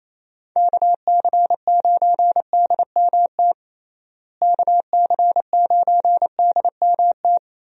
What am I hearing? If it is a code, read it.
Morse code, heard as a keyed tone, about 28 wpm, 700 Hz. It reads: KC9DMT KC9DMT
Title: USA Call Signs - 6 Characters - Repeated 2x - 28wpm